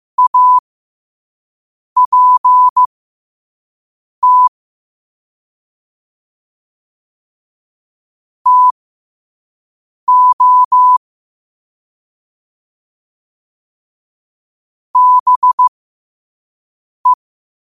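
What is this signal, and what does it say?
Morse code, heard as a keyed tone, about 15 wpm, spaced as for 5 wpm, 1000 Hz. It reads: APT TO BE
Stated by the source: Morse code sent at 5 wpm.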